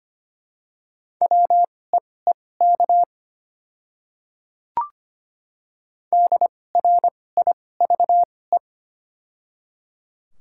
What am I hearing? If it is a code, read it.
Morse code, heard as a keyed tone, about 25 wpm, spaced as for 18 wpm, 700 Hz. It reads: WEEK DRIVE
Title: 2x Character Spacing - Top 500 Words - 25wpm